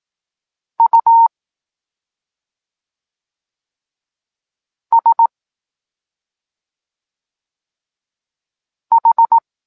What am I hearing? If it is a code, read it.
Morse code, heard as a keyed tone, about 18 wpm, spaced as for 2 wpm, 920 Hz. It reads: USH
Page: Tekst 8 – doskonalenie odbioru poznanych liter